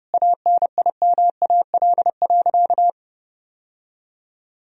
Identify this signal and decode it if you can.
Morse code, heard as a keyed tone, about 30 wpm, 700 Hz. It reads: ANIMAL.
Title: Sentences from Top 1000 Words - Review Part 1 - 30wpm